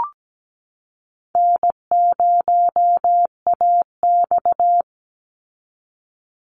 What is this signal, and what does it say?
Morse code, heard as a keyed tone, about 17 wpm, 700 Hz. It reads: N0AX
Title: Call Signs Repeated 2x - 17wpm